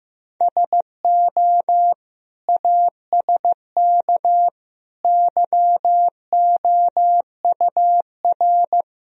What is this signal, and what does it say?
Morse code, heard as a keyed tone, about 15 wpm, 700 Hz. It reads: SO ASK YOUR
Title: Sentences from Top 1000 Words - Review Part 2 - 15wpm